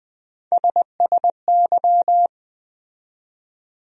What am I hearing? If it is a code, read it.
Morse code, heard as a keyed tone, about 20 wpm, 700 Hz. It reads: SSY
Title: USA Call Sign Postfixes - 3 Letters 20wpm